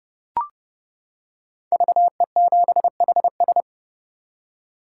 Morse code, heard as a keyed tone, about 30 wpm, 700 Hz. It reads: VE7HS
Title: K1USN SST QSO Practice - 30wpm